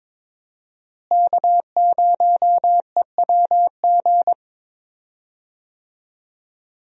Morse code, heard as a keyed tone, about 22 wpm, 700 Hz. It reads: K0EWG
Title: USA Call Signs - 5 Characters - Repeated 3x - 22wpm